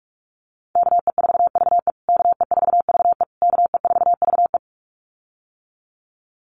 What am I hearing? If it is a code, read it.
Morse code, heard as a keyed tone, about 44 wpm, 690 Hz. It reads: KE4VE KE4VE KE4VE
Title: USA Call Signs Repeated 3x 45wpm